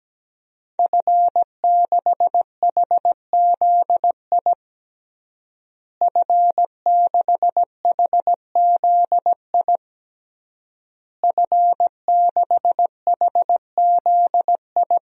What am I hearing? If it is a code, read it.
Morse code, heard as a keyed tone, about 17 wpm, 700 Hz. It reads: F6HZI F6HZI F6HZI
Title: DXCC-100 Call Signs - 5 Characters - Repeated 3x - 17wpm